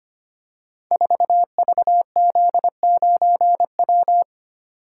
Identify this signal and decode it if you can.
Morse code, heard as a keyed tone, about 25 wpm, 700 Hz. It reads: 4VZ9W
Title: Code Groups - 5 Letter:Numbers - 25wpm